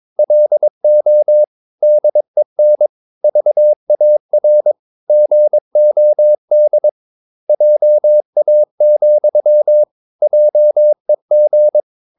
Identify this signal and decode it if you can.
Morse code, heard as a keyed tone, about 22 wpm, 600 Hz. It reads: LO DEN VAR GOD JA, JEG
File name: Ferie QRS 22wpm